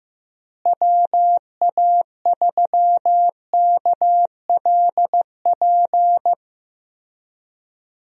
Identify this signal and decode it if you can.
Morse code, heard as a keyed tone, about 15 wpm, 700 Hz. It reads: WA3KLP
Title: USA Call Signs - 6 Characters - Repeated 2x - 15wpm